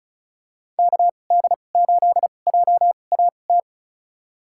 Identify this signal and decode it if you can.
Morse code, heard as a keyed tone, about 34 wpm, spaced as for 25 wpm, 700 Hz. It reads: KD8JAT
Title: USA Call Signs - 6 Characters - 2x Character Spacing - 35wpm